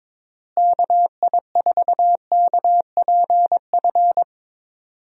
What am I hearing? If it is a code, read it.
Morse code, heard as a keyed tone, about 22 wpm, 700 Hz. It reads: KI4KPF
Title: USA Call Signs - 6 Characters - Repeated 3x - 22wpm